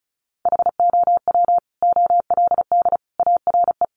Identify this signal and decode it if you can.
Morse code, heard as a keyed tone, about 34 wpm, 700 Hz. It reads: HOW OLD ARE
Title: Sentences from Top 200 Words 35wpm